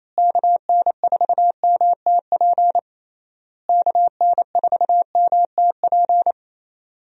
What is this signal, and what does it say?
Morse code, heard as a keyed tone, about 27 wpm, 700 Hz. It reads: KN4MTP KN4MTP
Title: USA Call Signs - 6 Characters - Repeated 3x - 28wpm